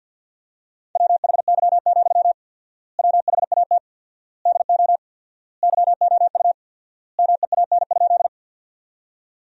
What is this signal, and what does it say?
Morse code, heard as a keyed tone, about 50 wpm, 700 Hz. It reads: WHY, WHAT DO YOU MEAN?
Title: 4x Word-Spacing - Sentences from Top 200 Words - 50wpm